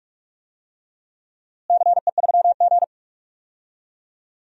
Morse code, heard as a keyed tone, about 48 wpm, 700 Hz. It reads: CE3G
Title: DXCC-100 Call Signs - 4 Characters - Repeated 2x - 45wpm